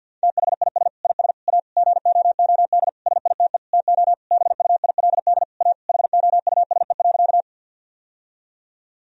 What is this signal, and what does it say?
Morse code, heard as a keyed tone, about 50 wpm, 700 Hz. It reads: THIS IS A GOOD SITE TO BUILD A HOUSE.
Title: Sentences from Top 1000 Words - Review Part 4 - 50wpm